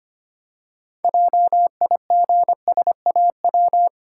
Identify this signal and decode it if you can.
Morse code, heard as a keyed tone, about 25 wpm, 700 Hz. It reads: JIGSAW
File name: Easy Words - Speed-Racing with Hints - 25wpm